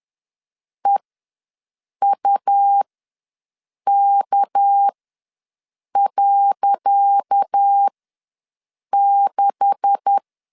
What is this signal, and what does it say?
Morse code, heard as a keyed tone, about 10 wpm, spaced as for 6 wpm, 800 Hz. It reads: EUK.6
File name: LEZIONI / 10 SIGNS / 10C DOT ws 10wpm 4213